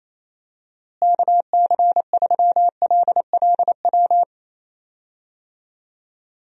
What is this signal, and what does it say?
Morse code, heard as a keyed tone, about 28 wpm, 700 Hz. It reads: KC3LLW
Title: USA Call Signs - 6 Characters - Repeated 3x - 28wpm